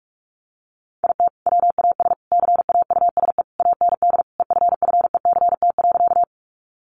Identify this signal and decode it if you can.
Morse code, heard as a keyed tone, about 44 wpm, 700 Hz. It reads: IT WAS CAUSE AND EFFECT.
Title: Sentences from Top 1000 Words 45wpm